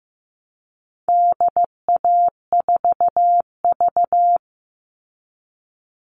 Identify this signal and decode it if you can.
Morse code, heard as a keyed tone, about 15 wpm, 700 Hz. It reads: DA4V
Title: All Cut Numbers - 4-Digit 15wpm